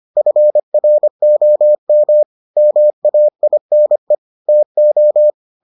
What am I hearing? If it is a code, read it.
Morse code, heard as a keyed tone, about 25 wpm, 600 Hz. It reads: FROM MAINE TO